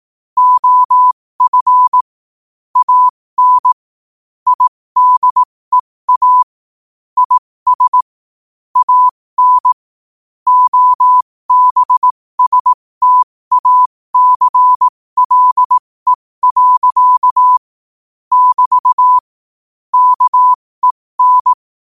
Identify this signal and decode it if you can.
Morse code, heard as a keyed tone, about 18 wpm, spaced as for 15 wpm, 1000 Hz. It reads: OF AN IDEA IS AN OBSTACLE. <BT> KEN
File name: qotdfiles / 15wpm